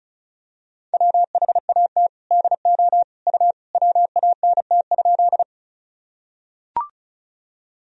Morse code, heard as a keyed tone, about 36 wpm, 700 Hz. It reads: WHAT DO U WANT?
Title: Sentences from Top 200 Words and Abbreviations 35wpm